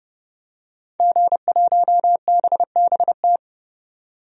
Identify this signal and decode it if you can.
Morse code, heard as a keyed tone, about 30 wpm, 690 Hz. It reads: G1BBT
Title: DXCC-100 Call Signs - 5 Characters - 30wpm